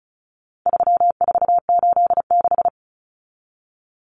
Morse code, heard as a keyed tone, about 34 wpm, 700 Hz. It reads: 3486
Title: Common Cut Numbers - 4-Digit 35wpm